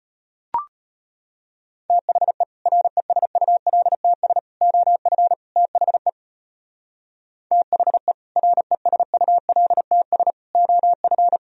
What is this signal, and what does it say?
Morse code, heard as a keyed tone, about 32 wpm, 700 Hz. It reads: THE RESULTS OF THE THE RESULTS OF